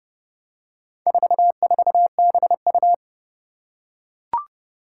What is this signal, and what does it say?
Morse code, heard as a keyed tone, about 31 wpm, 700 Hz. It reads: 44BU
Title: Four Challenging Letter-Number Combinations 30wpm